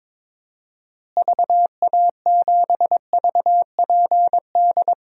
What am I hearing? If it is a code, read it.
Morse code, heard as a keyed tone, about 22 wpm, 700 Hz. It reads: VA7VPD
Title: DXCC-100 Call Signs - 6 Characters - Repeated 3x - 22wpm